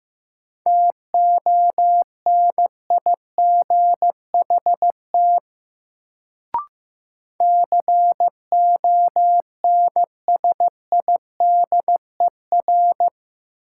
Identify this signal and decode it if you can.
Morse code, heard as a keyed tone, about 15 wpm, 700 Hz. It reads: TONIGHT CONSIDER
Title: Sentences from Top 600 Words - New Word Drill - 15wpm